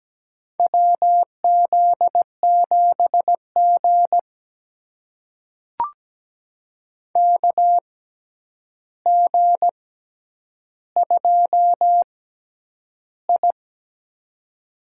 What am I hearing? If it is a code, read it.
Morse code, heard as a keyed tone, about 17 wpm, spaced as for 9 wpm, 700 Hz. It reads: WZ7G KG2I
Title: USA Call Signs - 4 Characters - 6x Character Spacing - 17wpm